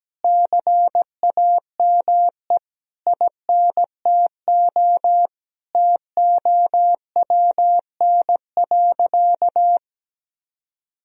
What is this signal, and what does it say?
Morse code, heard as a keyed tone, about 17 wpm, 700 Hz. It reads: CAME INTO TOWN.
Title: Sentences from Top 1000 Words - Review Part 2 - 17wpm